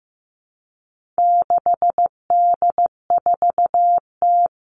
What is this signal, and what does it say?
Morse code, heard as a keyed tone, about 15 wpm, 700 Hz. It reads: 6D4T
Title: All Cut Numbers - 4-Digit 15wpm